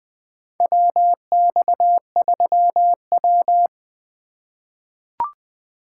Morse code, heard as a keyed tone, about 20 wpm, 700 Hz. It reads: WX3W
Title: USA Call Signs - 4 Characters - 3x Character Spacing - 20wpm